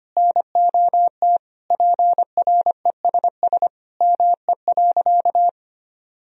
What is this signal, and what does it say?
Morse code, heard as a keyed tone, about 25 wpm, 700 Hz. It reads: NOT PRESS ME.
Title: Sentences from Top 1000 Words - Review Part 1 - 25wpm